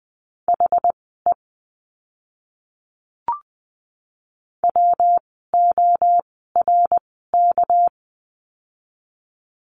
Morse code, heard as a keyed tone, about 20 wpm, spaced as for 14 wpm, 700 Hz. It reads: HE WORK
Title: 206 Lesson 33f - Letter j - Words 20wpm